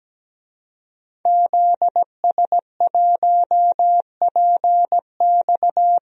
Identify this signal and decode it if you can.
Morse code, heard as a keyed tone, about 17 wpm, 700 Hz. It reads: ZS1PX
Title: DXCC-100 Call Signs - 5 Characters - Repeated 2x - 17wpm